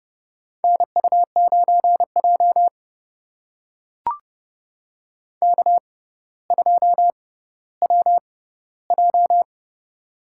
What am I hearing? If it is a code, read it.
Morse code, heard as a keyed tone, about 30 wpm, spaced as for 15 wpm, 700 Hz. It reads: NU9J K2WJ
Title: USA Call Signs - 4 Characters - 6x Character Spacing - 30wpm